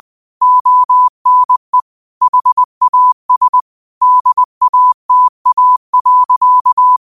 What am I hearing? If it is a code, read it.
Morse code, heard as a keyed tone, about 20 wpm, 1000 Hz. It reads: ONE HAS DATA.
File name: qotdfiles / 20wpm